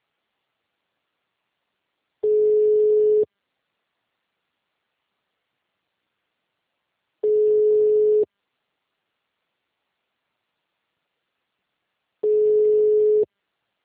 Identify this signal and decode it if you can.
Ringback tone (Europe, 425 Hz).